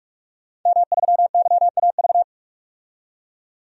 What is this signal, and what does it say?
Morse code, heard as a keyed tone, about 46 wpm, 700 Hz. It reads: M3YAV
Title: DXCC-100 Call Signs - 5 Characters - 45wpm